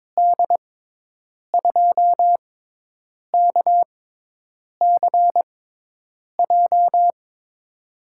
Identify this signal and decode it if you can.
Morse code, heard as a keyed tone, about 22 wpm, spaced as for 8 wpm, 700 Hz. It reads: D2KCJ
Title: USA Call Signs - 6 Characters - 6x Character Spacing - 22wpm